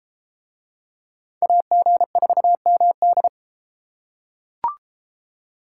Morse code, heard as a keyed tone, about 33 wpm, 700 Hz. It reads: AG4MD